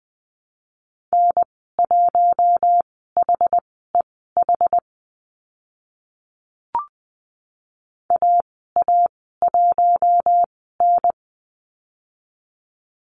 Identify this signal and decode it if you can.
Morse code, heard as a keyed tone, about 20 wpm, spaced as for 15 wpm, 700 Hz. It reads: N1HEH AA1N
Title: 062 Lesson 11g - Letter h - Call Signs 20wpm